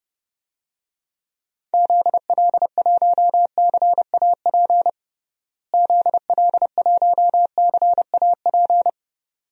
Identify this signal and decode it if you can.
Morse code, heard as a keyed tone, about 30 wpm, 700 Hz. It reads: ZL1CAP ZL1CAP
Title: DXCC-100 Call Signs - 6 Characters - Repeated 2x - 30wpm